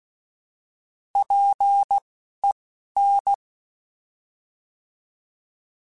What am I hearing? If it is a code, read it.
Morse code, heard as a keyed tone, about 16 wpm, spaced as for 11 wpm, 800 Hz. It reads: PEN
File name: Alphabetical / Alphabetical mp3s / Ps @10